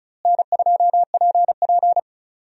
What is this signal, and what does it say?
Morse code, heard as a keyed tone, about 36 wpm, 700 Hz. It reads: N2PP
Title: USA Call Signs - 4 Characters - Repeated 2x - 35wpm